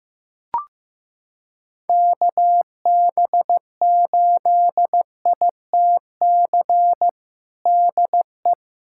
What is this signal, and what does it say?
Morse code, heard as a keyed tone, about 15 wpm, 700 Hz. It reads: KB8ITC DE
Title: USA Callsign DE Callsign - 6 Characters - 15wpm